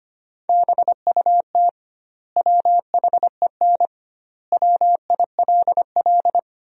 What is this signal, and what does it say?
Morse code, heard as a keyed tone, about 25 wpm, 700 Hz. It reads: BUT WHEN WILL
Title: 2x Word-Spacing - Sentences from Top 100 Words - 25wpm